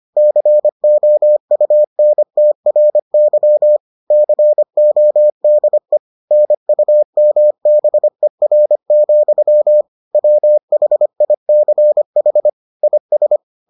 Morse code, heard as a keyed tone, about 25 wpm, 600 Hz. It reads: COUNTRY CODE NUMBER, WHICH IS